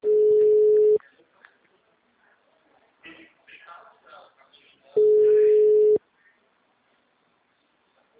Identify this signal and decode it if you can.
Ringback tone (Europe, 425 Hz), with speech.
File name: out- -2009- - - .25144